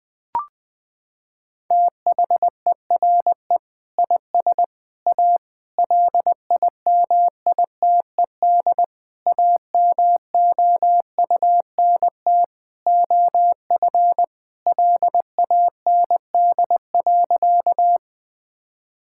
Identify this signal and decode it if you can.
Morse code, heard as a keyed tone, about 20 wpm, 700 Hz. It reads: THERE IS A LIMITED AMOUNT OF LAND.
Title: Sentences from Top 1000 Words - Review Part 6 - 20wpm